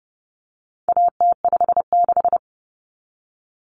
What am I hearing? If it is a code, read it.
Morse code, heard as a keyed tone, about 30 wpm, 700 Hz. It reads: AT56
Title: Common Cut Numbers - 4-Digit 30wpm